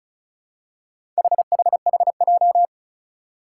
Morse code, heard as a keyed tone, about 36 wpm, 700 Hz. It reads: HHHJ
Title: Four Challenging Letter-Number Combinations 35wpm